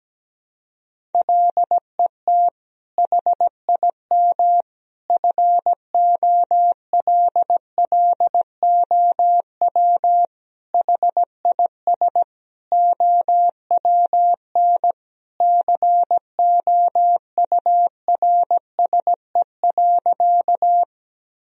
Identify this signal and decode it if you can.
Morse code, heard as a keyed tone, about 17 wpm, 700 Hz. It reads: LET HIM FOLLOW HIS OWN COURSE.
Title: Sentences from Top 1000 Words - Review Part 1 - 17wpm